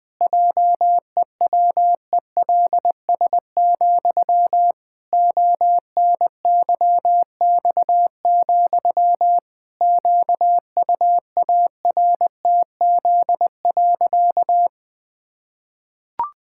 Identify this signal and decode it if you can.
Morse code, heard as a keyed tone, about 20 wpm, 700 Hz. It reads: JEWELS, ONYX, QUARTZ.
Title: Pangram Challenge v2 - 20wpm